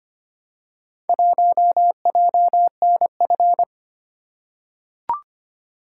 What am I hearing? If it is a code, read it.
Morse code, heard as a keyed tone, about 25 wpm, 700 Hz. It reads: 1JNF
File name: practice / 25wpm / Four Letter-Number Combinations - 25wpm